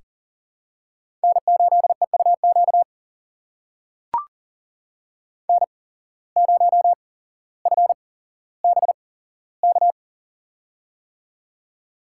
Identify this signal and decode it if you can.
Morse code, heard as a keyed tone, about 38 wpm, spaced as for 17 wpm, 700 Hz. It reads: N8EUQ N0FBK
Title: USA Call Signs - 5 Characters - 8x Character Spacing - 40wpm